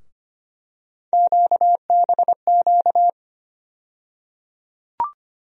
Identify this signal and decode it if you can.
Morse code, heard as a keyed tone, about 25 wpm, 700 Hz. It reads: QBQ
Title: USA Call Sign Postfixes - 3 Letters 25wpm